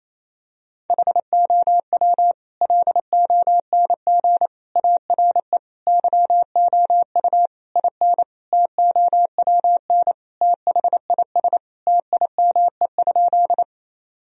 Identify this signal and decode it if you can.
Morse code, heard as a keyed tone, about 28 wpm, 700 Hz. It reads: HOW LONG ARE YOU IN TOWN THIS TIME?
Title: Sentences from Top 1000 Words - Review Part 2 - 28wpm